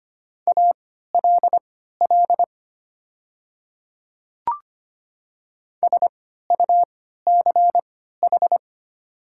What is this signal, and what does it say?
Morse code, heard as a keyed tone, about 25 wpm, spaced as for 14 wpm, 700 Hz. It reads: ALL SUCH
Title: 3x Character Spacing - Top 200 Words - 25wpm